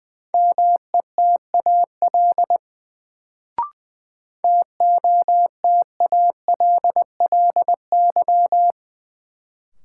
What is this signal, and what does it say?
Morse code, heard as a keyed tone, about 20 wpm, 700 Hz. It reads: METAL TOTALLY
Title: Sentences from Top 1600 Words - New Word Drill Alt - 20wpm